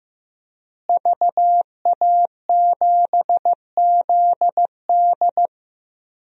Morse code, heard as a keyed tone, about 15 wpm, 700 Hz. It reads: VA7ZD
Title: DXCC-100 Call Signs - 5 Characters - Repeated 3x - 15wpm